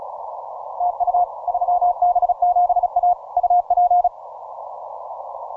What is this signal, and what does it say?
Morse code, heard as a keyed tone, about 34 wpm, 710 Hz. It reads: TU 3B7A UP